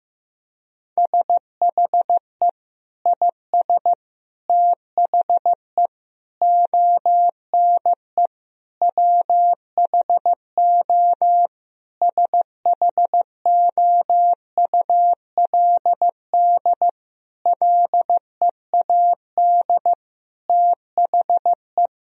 Morse code, heard as a keyed tone, about 15 wpm, 700 Hz. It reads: SHE IS THE ONE WHO SHOULD LEAD THE
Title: Sentences from Top 1000 Words - Review Part 2 - 15wpm